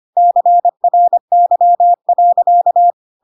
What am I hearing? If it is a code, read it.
Morse code, heard as a keyed tone, about 25 wpm, 700 Hz. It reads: CRY.
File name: Pinocchio / Pinocchio ch1 25 25 ik1hge